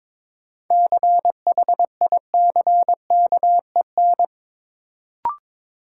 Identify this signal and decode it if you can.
Morse code, heard as a keyed tone, about 22 wpm, 700 Hz. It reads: CHICKEN
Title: Sentences from Top 1000 Words - Review Part 5 - New Word Drill - 22wpm